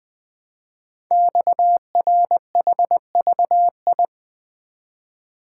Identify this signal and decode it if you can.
Morse code, heard as a keyed tone, about 20 wpm, 700 Hz. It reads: XRHVI
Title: Code Groups - 5 Letters - 20wpm